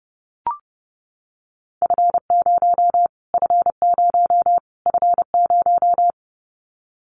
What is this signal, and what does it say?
Morse code, heard as a keyed tone, about 30 wpm, 700 Hz. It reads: F0 F0 F0
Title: Callsign Prefixes for Top 25 DXCC - Repeated 3x - 30wpm